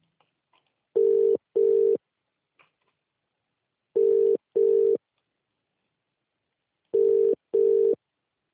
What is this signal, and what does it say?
Ringback tone (UK), 400 + 450 Hz.